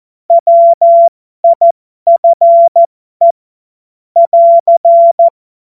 Morse code, heard as a keyed tone, about 14 wpm, spaced as for 12 wpm, 680 Hz. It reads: WIFE <AR>
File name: Morsekurs / Witze